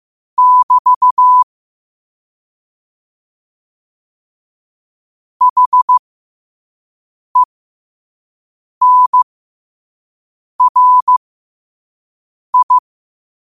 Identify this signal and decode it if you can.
Morse code, heard as a keyed tone, about 15 wpm, spaced as for 5 wpm, 1000 Hz. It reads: <BT> HENRI
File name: qotdfiles / 5wpm